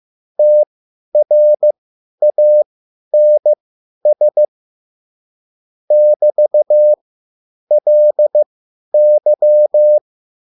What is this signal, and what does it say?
Morse code, heard as a keyed tone, about 15 wpm, spaced as for 10 wpm, 600 Hz. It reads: TRANS <BT>LY